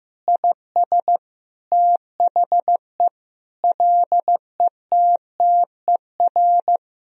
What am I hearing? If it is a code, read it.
Morse code, heard as a keyed tone, about 16 wpm, 700 Hz. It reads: IS THE LETTER
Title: Sentences from Top 1000 Words - Review Part 2 - 15wpm